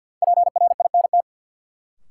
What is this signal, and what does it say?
Morse code, heard as a keyed tone, about 53 wpm, 700 Hz. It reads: PRINT
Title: Sentences from Top 1400 Words - New Word Drill Alt - 50wpm